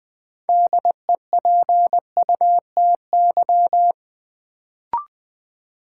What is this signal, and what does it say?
Morse code, heard as a keyed tone, about 20 wpm, 700 Hz. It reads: DEPUTY
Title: Build A Word - 2 Letters At A Time - 6-Letter Words - 20wpm